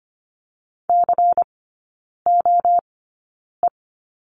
Morse code, heard as a keyed tone, about 25 wpm, spaced as for 9 wpm, 700 Hz. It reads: COE
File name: course / 25:10wpm / 093 Lesson 16e - Letter c - Three Characters 25wpm